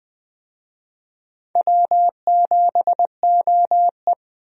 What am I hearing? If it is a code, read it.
Morse code, heard as a keyed tone, about 20 wpm, 700 Hz. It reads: W7OE
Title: USA Call Signs - 4 Characters - Repeated 3x - 20wpm